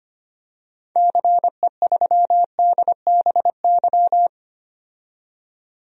Morse code, heard as a keyed tone, about 25 wpm, 700 Hz. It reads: CE3DBY
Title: DXCC-100 Call Signs - 6 Characters - 25wpm